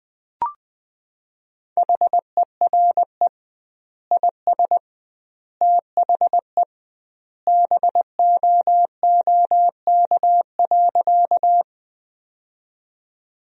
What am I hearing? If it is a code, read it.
Morse code, heard as a keyed tone, about 21 wpm, 700 Hz. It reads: HERE IS THE BOOK.
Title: 2x Word-Spacing - Sentences from Top 300 Words - 20wpm